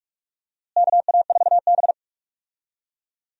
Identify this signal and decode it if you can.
Morse code, heard as a keyed tone, about 45 wpm, 700 Hz. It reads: KA4B
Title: USA Call Signs - 4 Characters - Repeated 3x - 45wpm